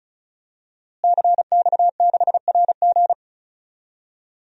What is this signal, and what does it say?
Morse code, heard as a keyed tone, about 36 wpm, 700 Hz. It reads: CX6RG
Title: DXCC-100 Call Signs - 5 Characters - Repeated 2x - 35wpm